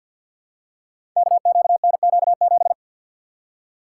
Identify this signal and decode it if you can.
Morse code, heard as a keyed tone, about 50 wpm, 700 Hz. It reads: KQNQ7
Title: Code Groups - 5 Letter:Numbers - 50wpm